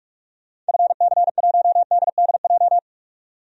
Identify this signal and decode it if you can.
Morse code, heard as a keyed tone, about 46 wpm, 700 Hz. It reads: FC1DDJ
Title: DXCC-100 Call Signs - 6 Characters - 45wpm